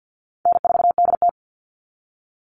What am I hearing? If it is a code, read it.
Morse code, heard as a keyed tone, about 50 wpm, 700 Hz. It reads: N4DT
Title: Easy Call Signs - 50wpm